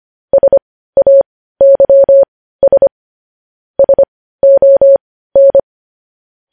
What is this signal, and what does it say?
Morse code, heard as a keyed tone, about 25 wpm, spaced as for 15 wpm, 550 Hz. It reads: SAYS SON
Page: news-headlines-in-morse-code-at-15-wpm